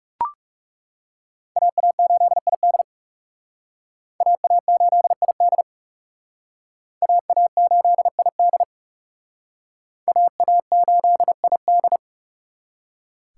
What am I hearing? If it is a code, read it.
Morse code, heard as a keyed tone, about 38 wpm, 700 Hz. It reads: AA8ID AA8ID AA8ID AA8ID